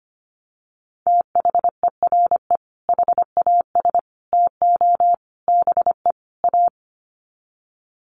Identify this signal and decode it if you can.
Morse code, heard as a keyed tone, about 25 wpm, 700 Hz. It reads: THERE HAS TO BE A
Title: Sets of 5 Words Encore 25wpm